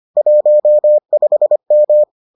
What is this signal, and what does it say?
Morse code, heard as a keyed tone, about 25 wpm, 600 Hz. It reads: 15M